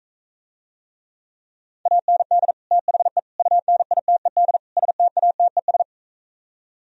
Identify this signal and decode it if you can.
Morse code, heard as a keyed tone, about 45 wpm, 700 Hz. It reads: AND THE UNITED STATES